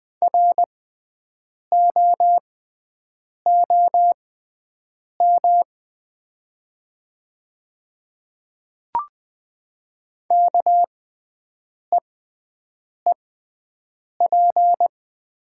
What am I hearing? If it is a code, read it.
Morse code, heard as a keyed tone, about 20 wpm, spaced as for 7 wpm, 700 Hz. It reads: ROOM KEEP
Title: 6x Character Spacing - Top 300 Words - 20wpm